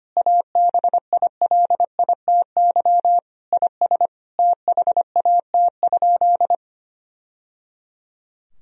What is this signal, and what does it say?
Morse code, heard as a keyed tone, about 25 wpm, 700 Hz. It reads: ABILITY IS THAT?
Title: Sentences from Top 1000 Words - Review Part 3 - 25wpm